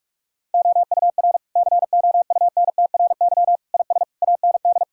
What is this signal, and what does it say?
Morse code, heard as a keyed tone, about 46 wpm, 700 Hz. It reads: OUR COUNTRY IS AND
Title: Sentences from Top 1000 Words - Review Part 6 - 45wpm